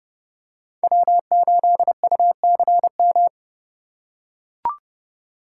Morse code, heard as a keyed tone, about 30 wpm, 700 Hz. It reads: W8UCM
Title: USA Call Signs - 5 Characters - 3x Character Spacing - 30wpm